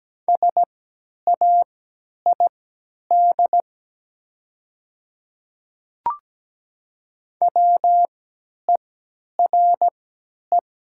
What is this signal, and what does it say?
Morse code, heard as a keyed tone, about 17 wpm, spaced as for 10 wpm, 700 Hz. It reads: SAID WERE
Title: 3x Character Spacing - Top 100 Words - 17wpm